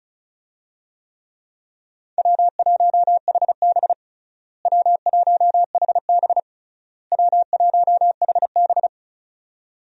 Morse code, heard as a keyed tone, about 34 wpm, 700 Hz. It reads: W1HB W1HB W1HB
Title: USA Call Signs - 4 Characters - Repeated 3x - 35wpm